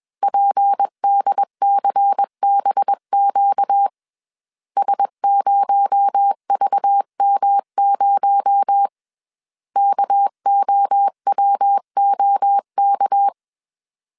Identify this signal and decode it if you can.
Morse code, heard as a keyed tone, about 21 wpm, 800 Hz. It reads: PDCBQ S0VM0 KOWOK
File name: CW / DET / 1 6 1500C S B 20wpm 2048